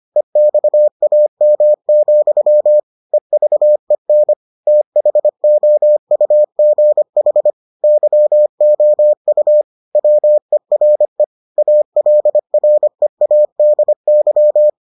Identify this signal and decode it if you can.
Morse code, heard as a keyed tone, about 25 wpm, 600 Hz. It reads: EXAM, EVEN THOUGH YOU WERE ALREADY